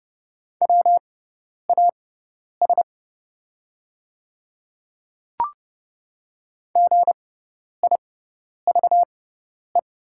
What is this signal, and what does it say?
Morse code, heard as a keyed tone, about 30 wpm, spaced as for 10 wpm, 700 Hz. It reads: WAS GIVE
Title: 6x Character Spacing - Top 500 Words - 30wpm